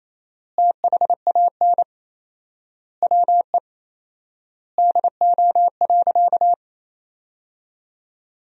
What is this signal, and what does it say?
Morse code, heard as a keyed tone, about 28 wpm, 700 Hz. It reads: THAN WE DO.
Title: 4x Word-Spacing - Sentences from Top 100 Words - 28wpm